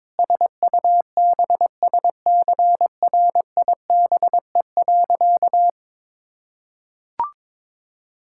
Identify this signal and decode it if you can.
Morse code, heard as a keyed tone, about 21 wpm, 700 Hz. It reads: SUBSCRIBE.
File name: practice / 22wpm / Sentences with 8-Word Phrases - 22wpm